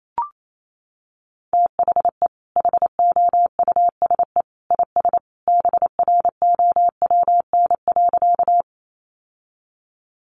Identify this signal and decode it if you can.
Morse code, heard as a keyed tone, about 28 wpm, 700 Hz. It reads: THE HOUSE IS BROWN.
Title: Sentences from Top 1200 Words 28wpm